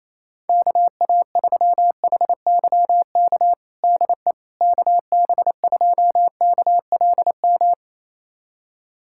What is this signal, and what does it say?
Morse code, heard as a keyed tone, about 27 wpm, 700 Hz. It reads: KA3HYK DE KB2KLM
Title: USA Callsign DE Callsign - 6 Characters - 28wpm